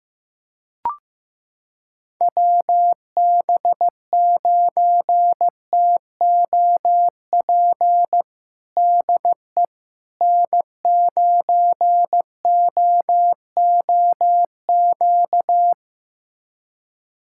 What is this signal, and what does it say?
Morse code, heard as a keyed tone, about 16 wpm, 700 Hz. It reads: WB9TOP DE N9OOQ
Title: USA Callsign DE Callsign 15wpm